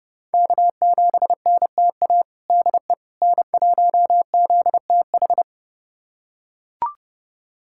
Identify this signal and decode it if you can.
Morse code, heard as a keyed tone, about 30 wpm, 700 Hz. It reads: K7NTA DE N1ZTH
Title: USA Callsign DE Callsign - 5 Characters - 30wpm